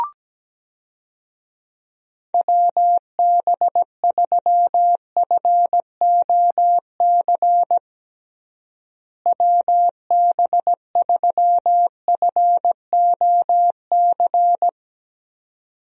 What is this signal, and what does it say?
Morse code, heard as a keyed tone, about 17 wpm, 700 Hz. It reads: WB3FOC WB3FOC
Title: USA Call Signs - 6 Characters - Repeated 2x - 17wpm